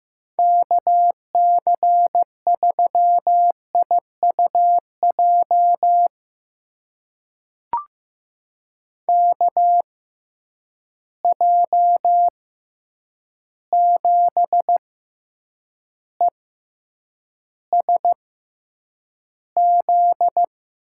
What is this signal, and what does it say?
Morse code, heard as a keyed tone, about 15 wpm, spaced as for 8 wpm, 700 Hz. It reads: KC3IUJ KJ7ESZ
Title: USA Call Signs - 6 Characters - 6x Character Spacing - 15wpm